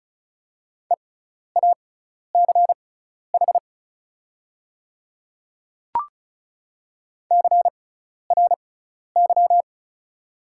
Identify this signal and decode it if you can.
Morse code, heard as a keyed tone, about 34 wpm, spaced as for 12 wpm, 690 Hz. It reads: EACH CRY
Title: 6x Character Spacing - Top 500 Words - 35wpm